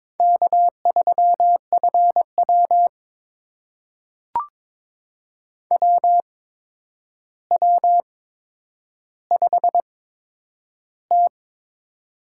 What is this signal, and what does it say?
Morse code, heard as a keyed tone, about 22 wpm, spaced as for 9 wpm, 700 Hz. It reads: K3FW WW5T
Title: USA Call Signs - 4 Characters - 8x Character Spacing - 22wpm